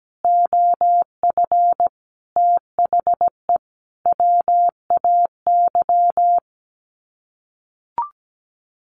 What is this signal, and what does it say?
Morse code, heard as a keyed tone, about 17 wpm, 700 Hz. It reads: OF THE WAY
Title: Sets of 5 Words Encore 17wpm